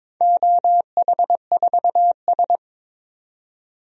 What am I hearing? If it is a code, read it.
Morse code, heard as a keyed tone, about 22 wpm, 700 Hz. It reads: OH4S
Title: DXCC-100 Call Signs - 4 Characters - 22wpm